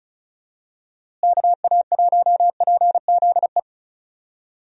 Morse code, heard as a keyed tone, about 36 wpm, 700 Hz. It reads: KA1PZE
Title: USA Call Signs - 6 Characters - Repeated 3x - 35wpm